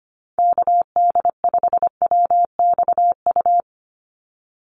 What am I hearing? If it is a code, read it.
Morse code, heard as a keyed tone, about 25 wpm, 700 Hz. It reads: KD5WXU
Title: USA Call Signs 25wpm